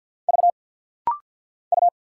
Morse code, heard as a keyed tone, about 50 wpm, 700 Hz. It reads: V U
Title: Single Letters - Rapid-Fire 50wpm